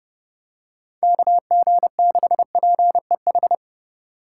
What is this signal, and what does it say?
Morse code, heard as a keyed tone, about 29 wpm, 700 Hz. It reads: KG6PEH